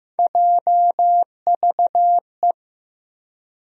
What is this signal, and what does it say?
Morse code, heard as a keyed tone, about 15 wpm, 700 Hz. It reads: JVE
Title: USA Call Sign Postfixes - 3 Letters 15wpm